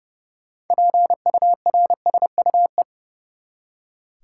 Morse code, heard as a keyed tone, about 30 wpm, 700 Hz. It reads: PURSUE
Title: Six Letter Words - 30wpm